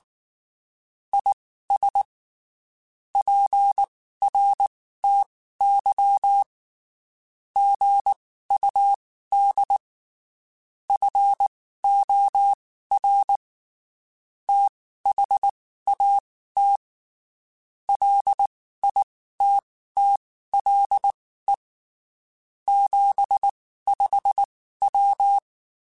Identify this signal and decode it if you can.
Morse code, heard as a keyed tone, about 19 wpm, spaced as for 14 wpm, 800 Hz. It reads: IS PRTY GUD FOR THAT LITTLE 75W
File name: (spaced) / QSOs Spaced-mp3s / WA7QOX 1 @18 sp